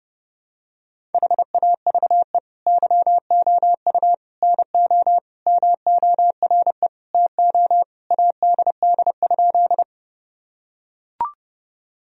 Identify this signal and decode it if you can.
Morse code, heard as a keyed tone, about 32 wpm, 700 Hz. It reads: HAVE YOU NO MORE TO ADD?
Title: Sentences from Top 1000 Words - Review Part 1 - 30wpm